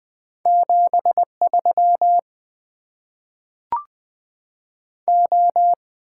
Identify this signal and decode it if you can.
Morse code, heard as a keyed tone, about 20 wpm, 700 Hz. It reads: 73 O
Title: Top 90 QSO Elements - 6x Character Spacing - 20wpm